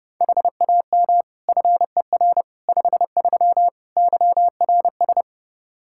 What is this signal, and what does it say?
Morse code, heard as a keyed tone, about 30 wpm, 700 Hz. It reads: HAM FER 53 YRS
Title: Practice QSO - Exchange 3 - More Info - 30wpm